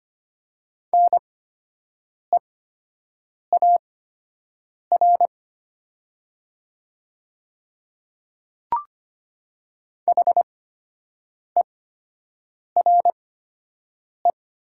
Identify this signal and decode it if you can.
Morse code, heard as a keyed tone, about 25 wpm, spaced as for 7 wpm, 700 Hz. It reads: NEAR HERE
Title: 8x Character Spacing - Top 200 Words - 25wpm